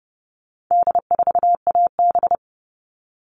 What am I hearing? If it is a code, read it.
Morse code, heard as a keyed tone, about 30 wpm, 700 Hz. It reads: D4AB
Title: All Cut Numbers - 4-Digit 30wpm